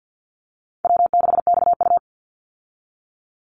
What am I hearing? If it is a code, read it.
Morse code, heard as a keyed tone, about 50 wpm, 700 Hz. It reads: W6XU
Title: Call Signs Repeated 3x - 50wpm